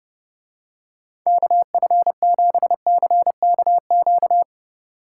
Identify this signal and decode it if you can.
Morse code, heard as a keyed tone, about 29 wpm, 700 Hz. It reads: KF7CKQ